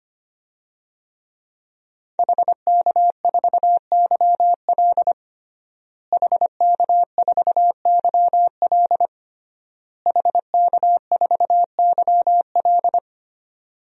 Morse code, heard as a keyed tone, about 25 wpm, 700 Hz. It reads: HK4YL HK4YL HK4YL
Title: DXCC-100 Call Signs - 5 Characters - Repeated 3x - 25wpm